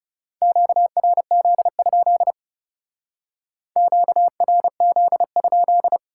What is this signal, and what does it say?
Morse code, heard as a keyed tone, about 39 wpm, 700 Hz. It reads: QRZ? QRZ?